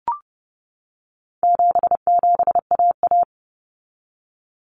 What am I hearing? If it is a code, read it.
Morse code, heard as a keyed tone, about 30 wpm, 700 Hz. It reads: 77AA
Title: Common Cut Numbers - 4-Digit 30wpm